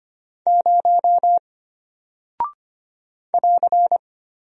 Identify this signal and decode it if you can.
Morse code, heard as a keyed tone, about 25 wpm, 700 Hz. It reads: 0 <AR>
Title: Single Character (Letter:Number:Punctuation:Prosign) Alt 25wpm